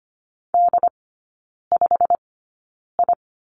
Morse code, heard as a keyed tone, about 25 wpm, spaced as for 9 wpm, 700 Hz. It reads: D5I
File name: course / 25:10wpm / 093 Lesson 16e - Letter c - Three Characters 25wpm